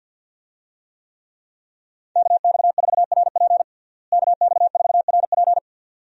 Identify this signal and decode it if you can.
Morse code, heard as a keyed tone, about 50 wpm, 700 Hz. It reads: KX4RP KX4RP
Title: USA Call Signs - 5 Characters - Repeated 2x - 50wpm